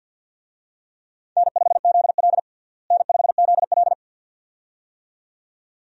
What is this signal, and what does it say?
Morse code, heard as a keyed tone, about 51 wpm, 700 Hz. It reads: N5ZL N5ZL
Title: USA Call Signs - 4 Characters - Repeated 2x - 50wpm